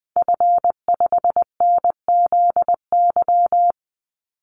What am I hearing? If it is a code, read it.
Morse code, heard as a keyed tone, about 20 wpm, 700 Hz. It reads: F5NZY
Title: Call Signs - 20wpm